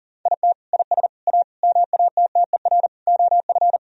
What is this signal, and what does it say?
Morse code, heard as a keyed tone, about 40 wpm, 700 Hz. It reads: IT IS A MATTER OF
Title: Sentences from Top 1000 Words - Review Part 3 - 40wpm